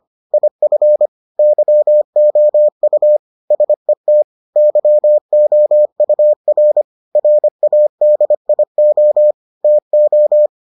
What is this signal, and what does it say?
Morse code, heard as a keyed tone, about 25 wpm, 600 Hz. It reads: IF YOU SET YOUR RADIO TO